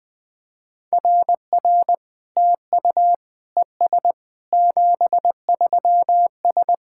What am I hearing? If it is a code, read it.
Morse code, heard as a keyed tone, about 20 wpm, 700 Hz. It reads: RR TU ES 73S
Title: SOTA QSO Practice 20wpm